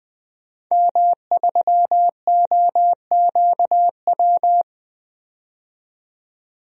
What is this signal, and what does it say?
Morse code, heard as a keyed tone, about 20 wpm, 700 Hz. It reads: M3OQW
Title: DXCC-100 Call Signs - 5 Characters - 20wpm